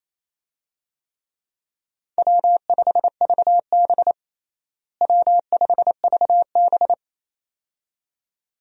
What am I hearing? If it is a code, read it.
Morse code, heard as a keyed tone, about 28 wpm, 700 Hz. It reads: W5VB W5VB
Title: USA Call Signs - 4 Characters - Repeated 2x - 28wpm